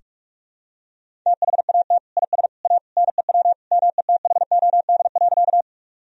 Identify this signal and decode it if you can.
Morse code, heard as a keyed tone, about 46 wpm, 700 Hz. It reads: THAT IS A NEW METHOD.
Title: Sentences from Top 1000 Words - Review Part 3 - 45wpm